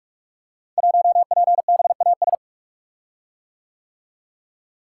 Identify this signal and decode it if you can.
Morse code, heard as a keyed tone, about 44 wpm, 690 Hz. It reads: 1PBAS
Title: Code Groups - 5 Letter:Numbers - 45wpm